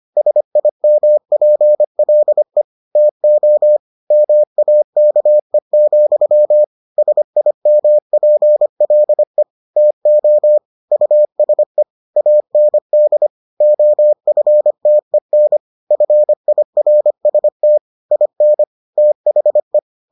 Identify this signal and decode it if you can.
Morse code, heard as a keyed tone, about 25 wpm, 600 Hz. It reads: SIMPLE TO MAKE, SIMPLE TO USE AND OFTEN FIRST IN THE